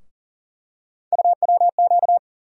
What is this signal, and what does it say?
Morse code, heard as a keyed tone, about 40 wpm, 700 Hz. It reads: UWQ
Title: USA Call Sign Postfixes - 3 Letters 40wpm